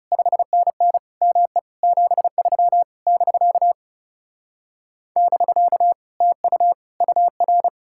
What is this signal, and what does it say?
Morse code, heard as a keyed tone, about 37 wpm, 700 Hz. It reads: 5NN ME 73 <BK> <BK> TU UR
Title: Speed-Racing Common QSO Phrases Encore - 30wpm